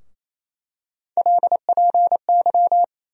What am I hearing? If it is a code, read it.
Morse code, heard as a keyed tone, about 28 wpm, 700 Hz. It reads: LPY
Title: USA Call Sign Postfixes - 3 Letters 28wpm